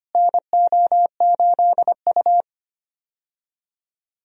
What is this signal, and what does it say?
Morse code, heard as a keyed tone, about 25 wpm, 700 Hz. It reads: NO8U
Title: USA Call Signs - 4 Characters - Repeated 3x - 25wpm